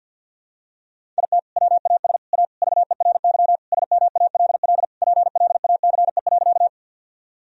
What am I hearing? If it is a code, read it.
Morse code, heard as a keyed tone, about 47 wpm, 700 Hz. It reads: IT WAS A VERY SMALL PLACE.
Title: Sentences from Top 1000 Words - Review Part 1 - 50wpm